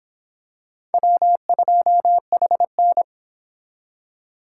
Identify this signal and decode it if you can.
Morse code, heard as a keyed tone, about 26 wpm, 700 Hz. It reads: W2HN